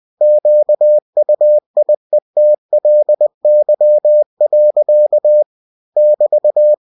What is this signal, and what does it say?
Morse code, heard as a keyed tone, about 20 wpm, 600 Hz. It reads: QUIETLY. <BT>
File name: cw-books / winnie-the-pooh / winnie-20wpm-page-0081